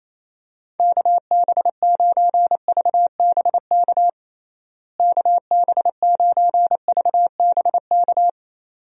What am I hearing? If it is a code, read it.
Morse code, heard as a keyed tone, about 28 wpm, 700 Hz. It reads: KB9VBK KB9VBK
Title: USA Call Signs - 6 Characters - Repeated 2x - 28wpm